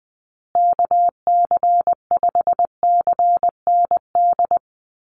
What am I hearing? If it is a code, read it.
Morse code, heard as a keyed tone, about 20 wpm, 700 Hz. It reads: KC5CND
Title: USA Call Signs Repeated 2x 20wpm